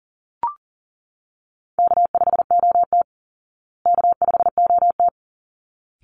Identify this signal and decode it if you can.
Morse code, heard as a keyed tone, about 40 wpm, 700 Hz. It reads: K5OT K5OT
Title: Easy Call Signs Repeated 2x - 40wpm